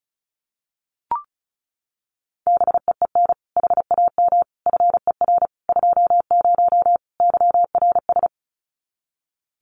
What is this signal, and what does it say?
Morse code, heard as a keyed tone, about 34 wpm, 700 Hz. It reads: BEEN HAM FER 20 YRS
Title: Common QSO Phrases 35wpm